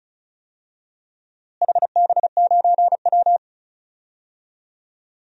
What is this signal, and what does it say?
Morse code, heard as a keyed tone, about 34 wpm, 700 Hz. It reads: HB9W
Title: DXCC-100 Call Signs - 4 Characters - Repeated 2x - 35wpm